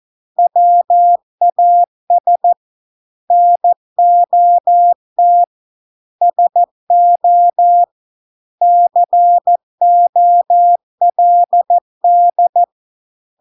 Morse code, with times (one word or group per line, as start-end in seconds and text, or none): 0.4-2.5 WAS
3.3-5.4 NOT
6.2-7.8 SO
8.6-12.6 COLD